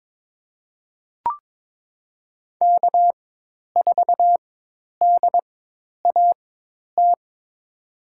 Morse code, heard as a keyed tone, about 22 wpm, spaced as for 10 wpm, 700 Hz. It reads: K4DAT
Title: USA Call Signs - 5 Characters - 4x Character Spacing - 22wpm